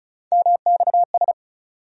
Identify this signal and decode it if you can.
Morse code, heard as a keyed tone, about 36 wpm, 700 Hz. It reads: MXS
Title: USA Call Sign Postfixes - 3 Letters 35wpm